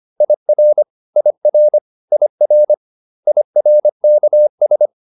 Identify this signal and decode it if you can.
Morse code, heard as a keyed tone, about 25 wpm, 600 Hz. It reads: IR IR IR IRKS